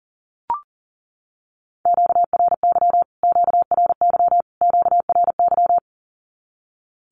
Morse code, heard as a keyed tone, about 40 wpm, 700 Hz. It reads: QRY QRY QRY
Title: Q-Codes Repeated 3x 40wpm